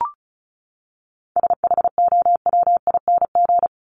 Morse code, heard as a keyed tone, about 34 wpm, 700 Hz. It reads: SHOWING
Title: ING Suffix - Top 100 Words with Base Words 35wpm